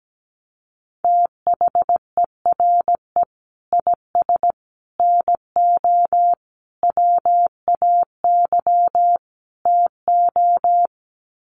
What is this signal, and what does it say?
Morse code, heard as a keyed tone, about 17 wpm, 700 Hz. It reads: THERE IS NO WAY TO
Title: Sets of 5 Words Encore 17wpm